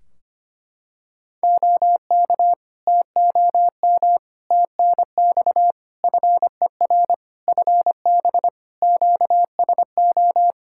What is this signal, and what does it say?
Morse code, heard as a keyed tone, about 25 wpm, 700 Hz. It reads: OK TOM TNX FER FB QSO
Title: Practice QSO - Exchange 4 - Ending - 25wpm